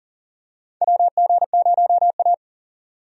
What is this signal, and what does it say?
Morse code, heard as a keyed tone, about 40 wpm, 700 Hz. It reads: WG0A
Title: USA Call Signs - 4 Characters - 40wpm